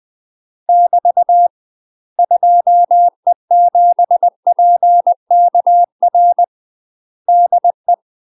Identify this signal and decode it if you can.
Morse code, heard as a keyed tone, about 20 wpm, 700 Hz. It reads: <BT> 2E7PKR DE
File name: UK MASTER QSOs TRAINING / 20WPM / UK MASTER QSOs 0045